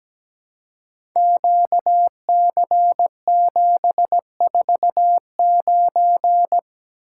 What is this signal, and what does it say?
Morse code, heard as a keyed tone, about 17 wpm, 700 Hz. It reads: QC749
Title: Code Groups - 5 Letter:Numbers - 17wpm